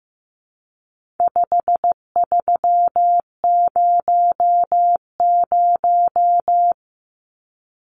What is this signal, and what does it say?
Morse code, heard as a keyed tone, about 15 wpm, 700 Hz. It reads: 5300
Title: Numbers - 4-Digit 15wpm